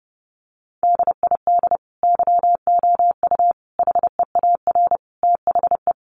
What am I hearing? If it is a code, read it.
Morse code, heard as a keyed tone, about 30 wpm, 700 Hz. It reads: DID YOU HEAR THE
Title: Sentences from Top 1100 Words 30wpm